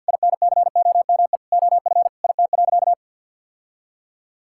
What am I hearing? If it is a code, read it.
Morse code, heard as a keyed tone, about 51 wpm, 700 Hz. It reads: INCOME OF IT.
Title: Sentences from Top 1000 Words - Review Part 6 - 50wpm